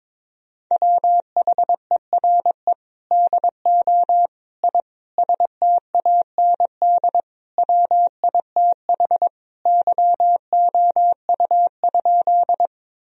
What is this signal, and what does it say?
Morse code, heard as a keyed tone, about 22 wpm, 700 Hz. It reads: WHERE DO I STAND WITH YOU?
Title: Sentences from Top 1000 Words - Review Part 1 - 22wpm